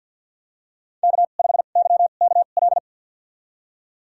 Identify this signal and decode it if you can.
Morse code, heard as a keyed tone, about 52 wpm, spaced as for 36 wpm, 700 Hz. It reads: K5YKL
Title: USA Call Signs - 5 Characters - 2x Character Spacing - 50wpm